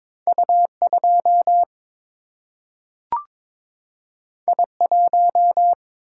Morse code, heard as a keyed tone, about 21 wpm, 700 Hz. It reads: U2 I1
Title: Single-Letter Single-Number 22wpm